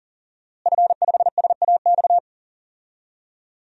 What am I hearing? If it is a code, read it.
Morse code, heard as a keyed tone, about 40 wpm, 700 Hz. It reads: F5SAX
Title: DXCC-100 Call Signs - 5 Characters - Repeated 3x - 40wpm